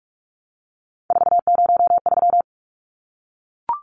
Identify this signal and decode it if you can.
Morse code, heard as a keyed tone, about 44 wpm, 700 Hz. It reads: 403
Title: Numbers - 3-Digit 45wpm